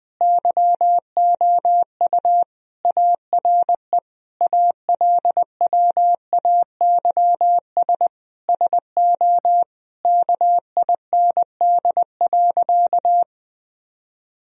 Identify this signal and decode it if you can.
Morse code, heard as a keyed tone, about 21 wpm, 700 Hz. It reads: YOU ARE ALWAYS SO KIND.
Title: Sentences from Top 1000 Words - Review Part 3 - 20wpm